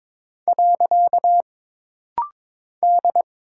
Morse code, heard as a keyed tone, about 22 wpm, 700 Hz. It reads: . D
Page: This rapid-fire timing is geared to push your ICR (Instant Character Recognition) skills.